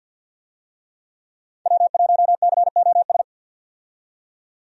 Morse code, heard as a keyed tone, about 50 wpm, 700 Hz. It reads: W1COS
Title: USA Call Signs - 5 Characters - 50wpm